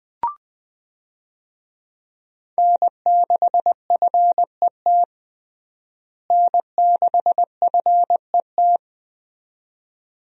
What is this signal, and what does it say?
Morse code, heard as a keyed tone, about 21 wpm, 700 Hz. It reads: N6FET N6FET
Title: USA Call Signs - 5 Characters - Repeated 2x - 20wpm